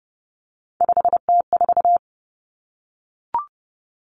Morse code, heard as a keyed tone, about 30 wpm, 700 Hz. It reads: 5T4
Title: Common Cut Numbers - 3-Digit 30wpm